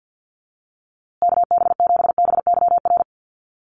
Morse code, heard as a keyed tone, about 50 wpm, 700 Hz. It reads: KB7BYR
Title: USA Call Signs Repeated 2x 50wpm